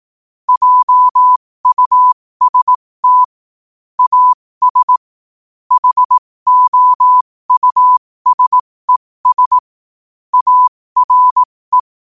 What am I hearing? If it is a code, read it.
Morse code, heard as a keyed tone, about 18 wpm, spaced as for 15 wpm, 1000 Hz. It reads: JUST AS HOUSES ARE
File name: qotdfiles / 15wpm